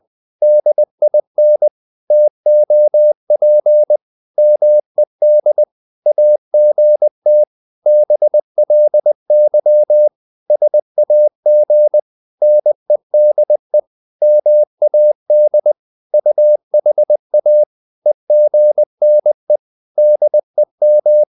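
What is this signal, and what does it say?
Morse code, heard as a keyed tone, about 20 wpm, 600 Hz. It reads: DIN TOP MED AGT BLY SAG NEDE MAD UHA EGNE DEM
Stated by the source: Morse code practice of short Danish words at 20 wpm, from CW med Gnister.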